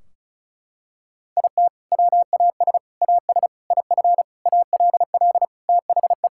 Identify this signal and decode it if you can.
Morse code, heard as a keyed tone, about 36 wpm, 700 Hz. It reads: IT WAS AS IF ALL THE
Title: Sentences from Top 1600 Words 35wpm